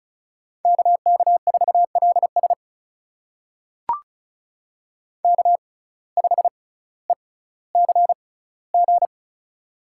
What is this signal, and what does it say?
Morse code, heard as a keyed tone, about 36 wpm, spaced as for 18 wpm, 700 Hz. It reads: KK4LS K5ECG
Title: USA Call Signs - 5 Characters - 6x Character Spacing - 35wpm